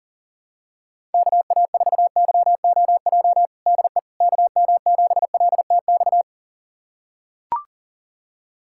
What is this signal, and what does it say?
Morse code, heard as a keyed tone, about 40 wpm, 700 Hz. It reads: KA4YOJ DE KM7LTX
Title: USA Callsign DE Callsign 40wpm